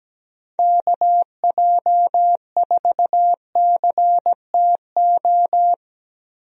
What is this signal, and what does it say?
Morse code, heard as a keyed tone, about 17 wpm, 700 Hz. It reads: KJ4CTO
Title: USA Call Signs - 6 Characters - Repeated 2x - 17wpm